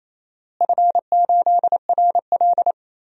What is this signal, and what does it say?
Morse code, heard as a keyed tone, about 28 wpm, 700 Hz. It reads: F8RL
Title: DXCC-100 Call Signs - 4 Characters - Repeated 2x - 28wpm